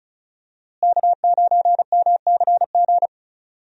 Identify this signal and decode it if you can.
Morse code, heard as a keyed tone, about 36 wpm, 700 Hz. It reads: K9MCG
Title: USA Call Signs - 5 Characters - Repeated 2x - 35wpm